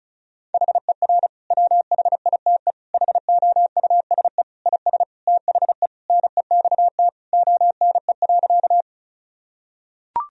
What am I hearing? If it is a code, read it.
Morse code, heard as a keyed tone, about 36 wpm, 700 Hz. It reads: HER WHITE HOUSE IS THE NEXT ONE.
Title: Sentences from Top 1000 Words - Review Part 1 - 35wpm